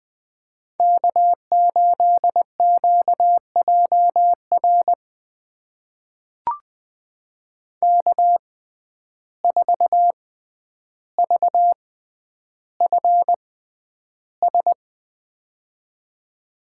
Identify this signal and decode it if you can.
Morse code, heard as a keyed tone, about 20 wpm, spaced as for 10 wpm, 700 Hz. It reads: K8QJR K4VFS
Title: USA Call Signs - 5 Characters - 6x Character Spacing - 20wpm